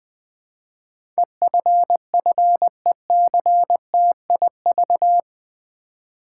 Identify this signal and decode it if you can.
Morse code, heard as a keyed tone, about 20 wpm, 700 Hz. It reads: EFFECTIV
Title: Build A Word - 4 Letters At A Time - 11-Letter Words - 20wpm